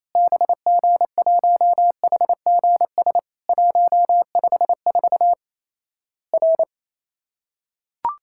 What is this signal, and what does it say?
Morse code, heard as a keyed tone, about 28 wpm, 700 Hz. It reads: BG1HGS 154 R
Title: Worked All Europe DX Contest - QTC Exchange - 28wpm